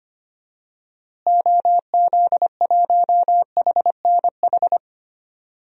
Morse code, heard as a keyed tone, about 25 wpm, 700 Hz. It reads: OZ1HNH
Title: DXCC-100 Call Signs - 6 Characters - Repeated 3x - 25wpm